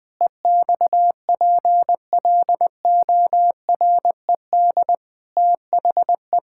Morse code, heard as a keyed tone, about 21 wpm, 700 Hz. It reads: EXPLORED THE
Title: Sentences with 6-Word Phrases - Spoken Beginning - 20wpm